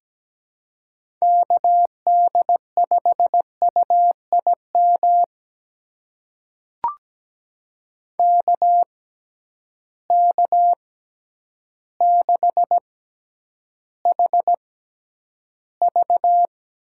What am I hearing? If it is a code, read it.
Morse code, heard as a keyed tone, about 17 wpm, spaced as for 9 wpm, 700 Hz. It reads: KD5UIM KK6HV
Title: USA Call Signs - 6x Character Spacing - 17wpm